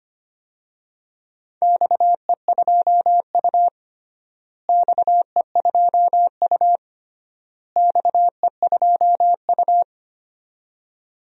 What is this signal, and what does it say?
Morse code, heard as a keyed tone, about 25 wpm, 700 Hz. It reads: XE2U XE2U XE2U
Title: DXCC-100 Call Signs - 4 Characters - Repeated 3x - 25wpm